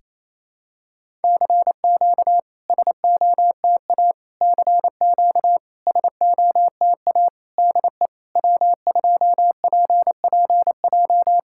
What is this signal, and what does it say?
Morse code, heard as a keyed tone, about 27 wpm, 660 Hz. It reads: CQ SOTA CQ SOTA DE W2PPJ
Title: SOTA QSO Practice 28wpm